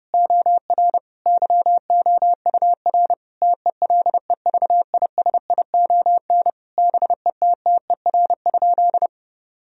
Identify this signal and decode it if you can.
Morse code, heard as a keyed tone, about 32 wpm, 700 Hz. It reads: OR YOUR TELEVISION BETTER?
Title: Sentences from Top 1000 Words - Review Part 4 - 30wpm